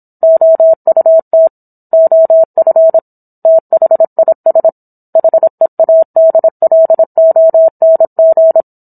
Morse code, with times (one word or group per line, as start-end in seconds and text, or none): 0.2-1.5 OUT
1.9-3.0 OF
3.5-4.7 THIS
5.2-8.6 HEADLONG